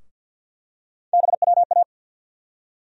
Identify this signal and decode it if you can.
Morse code, heard as a keyed tone, about 53 wpm, 700 Hz. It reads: BWA
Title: USA Call Sign Postfixes - 3 Letters 50wpm